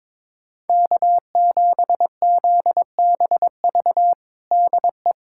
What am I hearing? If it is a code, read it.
Morse code, heard as a keyed tone, about 22 wpm, 700 Hz. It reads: K7ZBV DE
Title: USA Callsign DE Callsign - 5 Characters - 22wpm